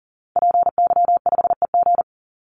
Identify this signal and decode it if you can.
Morse code, heard as a keyed tone, about 40 wpm, 700 Hz. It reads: PY5EG
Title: Call Signs - 40wpm